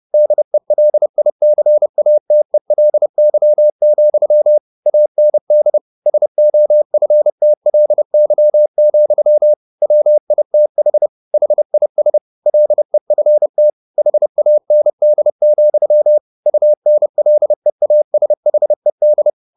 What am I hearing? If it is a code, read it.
Morse code, heard as a keyed tone, about 30 wpm, 600 Hz. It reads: DELICATELY, AND SOFTLY, WITH HIS LEFT HAND, UNLEASHED
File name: the old man and the sea / the old man and the sea 1 30wpm